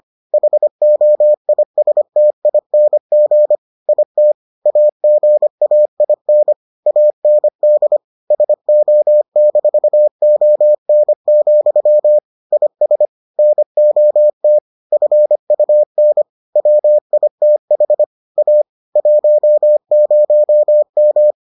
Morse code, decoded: HOISTING IT AGAIN AND SO-ON, IS NOT FUN WITH A 10M

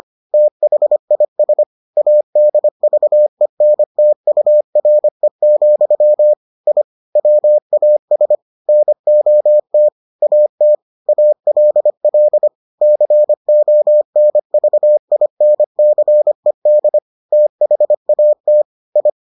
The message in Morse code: THIS ADVENTURE, I WAS NOT AT ALL CONVINCED THAT I

